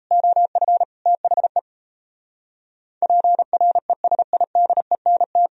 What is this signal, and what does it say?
Morse code, heard as a keyed tone, about 31 wpm, 700 Hz. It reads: OF THE PRESIDENT
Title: Speed-Racing - Sets of 3 Words 28wpm